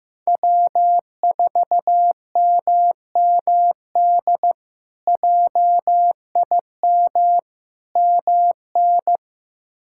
Morse code, heard as a keyed tone, about 15 wpm, 700 Hz. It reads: W4MMD JIM MN
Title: K1USN SST QSO Practice - 15wpm